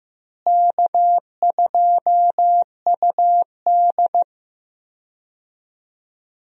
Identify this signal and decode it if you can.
Morse code, heard as a keyed tone, about 15 wpm, 700 Hz. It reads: K2UD
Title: USA Call Signs - 4 Characters - Repeated 2x - 15wpm